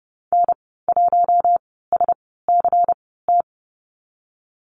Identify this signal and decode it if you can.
Morse code, heard as a keyed tone, about 31 wpm, 700 Hz. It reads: N 1 S C T
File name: Lesson 19g - Letter w - Call Signs 30wpm